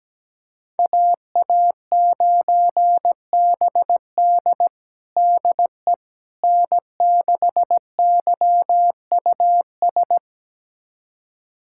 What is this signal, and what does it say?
Morse code, heard as a keyed tone, about 17 wpm, 700 Hz. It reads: AA9BD DE N6YUS
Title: USA Callsign DE Callsign 17wpm